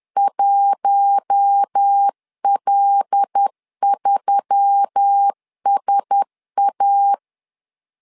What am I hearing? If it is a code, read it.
Morse code, heard as a keyed tone, about 10 wpm, 800 Hz. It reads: 1L3SA
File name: LEZIONI / 5 LBVC / 5 L 10wpm 2259